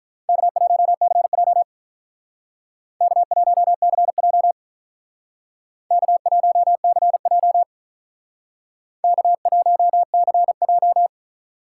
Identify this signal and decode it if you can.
Morse code, heard as a keyed tone, about 45 wpm, 700 Hz. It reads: K1CJ K1CJ K1CJ K1CJ